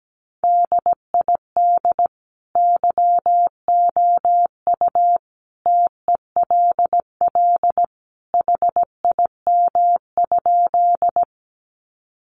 Morse code, heard as a keyed tone, about 17 wpm, 700 Hz. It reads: DID YOU TELL HIM?
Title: Sentences from Top 200 Words 17wpm